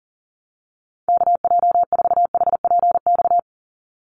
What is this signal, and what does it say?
Morse code, heard as a keyed tone, about 40 wpm, 700 Hz. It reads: KJ4HPX
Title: USA Call Signs 40wpm